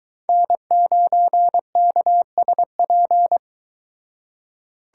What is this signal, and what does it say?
Morse code, heard as a keyed tone, about 22 wpm, 700 Hz. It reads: N9KSP